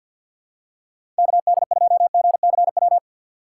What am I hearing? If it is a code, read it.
Morse code, heard as a keyed tone, about 53 wpm, 700 Hz. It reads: KD2GCW
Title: USA Call Signs - 2x Character Spacing - 50wpm